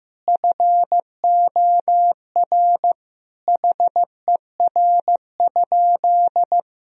Morse code, heard as a keyed tone, about 16 wpm, 700 Hz. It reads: FOR HER?
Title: Sentences from Top 1000 Words - Review Part 1 - 15wpm